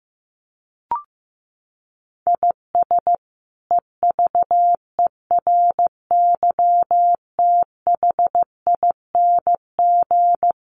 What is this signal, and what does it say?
Morse code, heard as a keyed tone, about 15 wpm, 700 Hz. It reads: IS EVERYTHING
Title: Sentences from Top 700 Words 15wpm